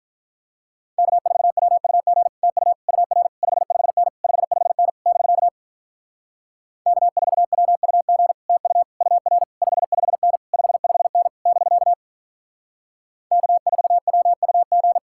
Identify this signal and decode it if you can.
Morse code, heard as a keyed tone, about 44 wpm, 700 Hz. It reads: K4WUG TU UR 55N 55N <BK> K4WUG TU UR 55N 55N <BK> K4WUG